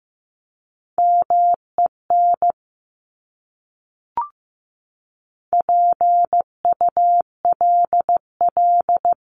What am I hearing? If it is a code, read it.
Morse code, heard as a keyed tone, about 15 wpm, 700 Hz. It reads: MEN PULL
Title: Top 500 Words Repeated 15wpm